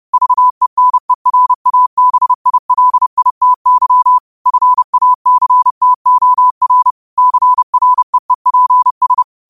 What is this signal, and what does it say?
Morse code, heard as a keyed tone, about 30 wpm, 1000 Hz. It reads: VENERABILITY FACTOR CREEPS